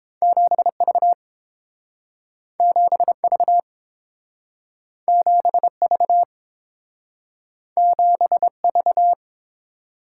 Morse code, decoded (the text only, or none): 7V 7V 7V 7V